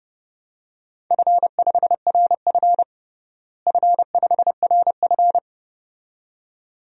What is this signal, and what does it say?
Morse code, heard as a keyed tone, about 30 wpm, 700 Hz. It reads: F5RF F5RF
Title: DXCC-100 Call Signs - 4 Characters - Repeated 2x - 30wpm